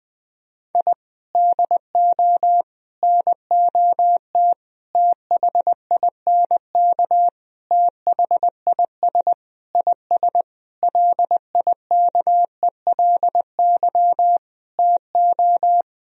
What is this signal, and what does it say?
Morse code, heard as a keyed tone, about 20 wpm, 700 Hz. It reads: I DO NOT THINK THIS IS LIKELY TO
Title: Sentences from Top 1000 Words - Review Part 4 - 20wpm